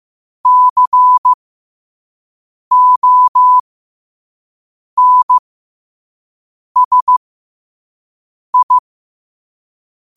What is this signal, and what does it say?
Morse code, heard as a keyed tone, about 15 wpm, spaced as for 5 wpm, 1000 Hz. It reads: CONSI
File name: qotdfiles / 5wpm